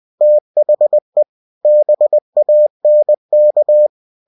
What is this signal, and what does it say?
Morse code, heard as a keyed tone, about 20 wpm, 600 Hz. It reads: THE BANK
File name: cw-books / winnie-the-pooh / winnie-20wpm-page-0013